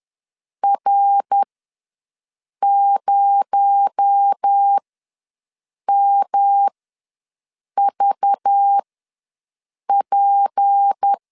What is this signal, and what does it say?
Morse code, heard as a keyed tone, about 10 wpm, spaced as for 6 wpm, 800 Hz. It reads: R0MVP
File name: LEZIONI / 6 PKX0 / 6 PKX0 ws 10wpm 4220